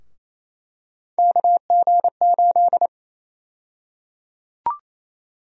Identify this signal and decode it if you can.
Morse code, heard as a keyed tone, about 27 wpm, 700 Hz. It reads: KG8
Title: USA Call Sign Prefixes - Repeated 2x - 28wpm